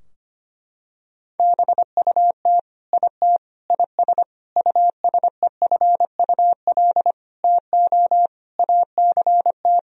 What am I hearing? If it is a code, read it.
Morse code, heard as a keyed tone, about 25 wpm, 700 Hz. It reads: BUT IT IS USEFUL TO ACT
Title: Sentences from Top 1000 Words - Review Part 4 - 25wpm